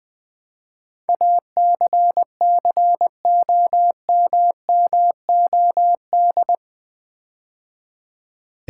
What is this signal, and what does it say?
Morse code, heard as a keyed tone, about 20 wpm, 700 Hz. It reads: ACCOMMOD
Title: Build A Word - 4 Letters At A Time - 11-Letter Words - 20wpm